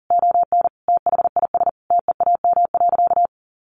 Morse code, heard as a keyed tone, about 37 wpm, 700 Hz. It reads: ON THIS TEAM.
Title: Sentences from Top 1500 Words 40wpm